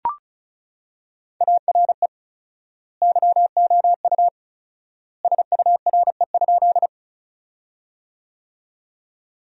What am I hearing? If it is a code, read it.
Morse code, heard as a keyed tone, about 34 wpm, 700 Hz. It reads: ARE YOU SURE?
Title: 4x Word-Spacing - Sentences from Top 300 Words - 35wpm